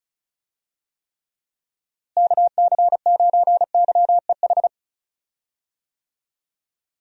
Morse code, heard as a keyed tone, about 36 wpm, 700 Hz. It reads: KC9YEH
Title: USA Call Signs - 6 Characters - Repeated 2x - 35wpm